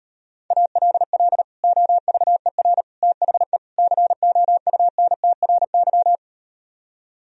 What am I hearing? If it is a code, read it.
Morse code, heard as a keyed tone, about 38 wpm, 700 Hz. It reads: ALL OVER THE COUNTRY